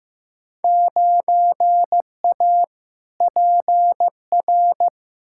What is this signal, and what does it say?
Morse code, heard as a keyed tone, about 16 wpm, 700 Hz. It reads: 9A PR
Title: Field Day Exchanges Encore - 15wpm